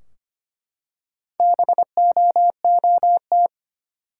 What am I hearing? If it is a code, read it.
Morse code, heard as a keyed tone, about 25 wpm, 700 Hz. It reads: BOOT
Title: Four Letter Words - 25wpm